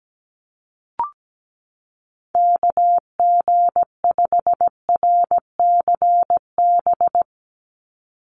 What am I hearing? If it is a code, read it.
Morse code, heard as a keyed tone, about 17 wpm, 700 Hz. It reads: KG5RCB
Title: USA Call Signs 17wpm